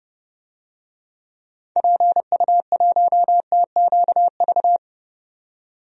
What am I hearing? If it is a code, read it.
Morse code, heard as a keyed tone, about 30 wpm, 700 Hz. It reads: PU1TQV
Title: DXCC-100 Call Signs - 6 Characters - 30wpm